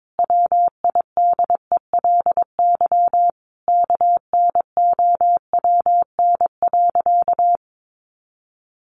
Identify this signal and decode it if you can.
Morse code, heard as a keyed tone, about 22 wpm, 700 Hz. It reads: WIDELY KNOWN.
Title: Sentences from Top 1200 Words 22wpm